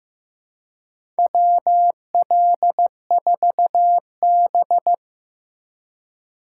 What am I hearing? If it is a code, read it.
Morse code, heard as a keyed tone, about 15 wpm, 700 Hz. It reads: WL4B
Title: USA Call Signs - 3x Character Spacing - 15wpm